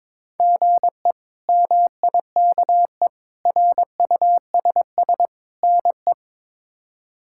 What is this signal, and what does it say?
Morse code, heard as a keyed tone, about 22 wpm, 700 Hz. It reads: GE MIKE RUSS NE
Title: K1USN SST QSO Practice - 22wpm